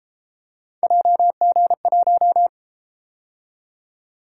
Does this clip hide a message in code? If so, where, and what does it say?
Yes: Morse code from 0.8 to 2.5 s, heard as a keyed tone, about 31 wpm, 700 Hz. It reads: JG1